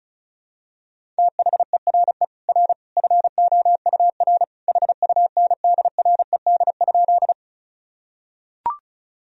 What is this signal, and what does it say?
Morse code, heard as a keyed tone, about 36 wpm, 700 Hz. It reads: THERE R FOUR HUNDRED?
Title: Sentences from Top 400 Words and Abbreviations 35wpm